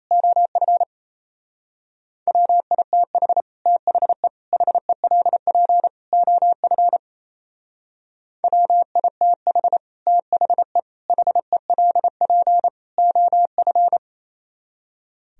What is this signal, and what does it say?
Morse code, heard as a keyed tone, about 37 wpm, 700 Hz. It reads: OF WITH THE HELP OF WITH THE HELP OF